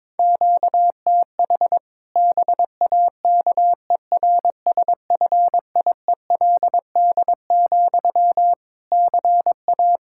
Morse code, heard as a keyed tone, about 22 wpm, 700 Hz. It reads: QTH BAKERSFIELD, CA
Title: Practice QSO - Exchange 1 - Beginning - 22wpm